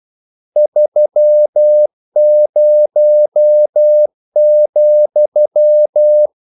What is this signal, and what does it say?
Morse code, heard as a keyed tone, about 12 wpm, 600 Hz. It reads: 30,
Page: English quotations in CW (telegraphy - morse)